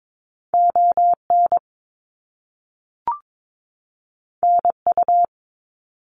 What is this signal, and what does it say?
Morse code, heard as a keyed tone, about 22 wpm, 700 Hz. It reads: ON NU
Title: Canadian Provinces and Territories 22wpm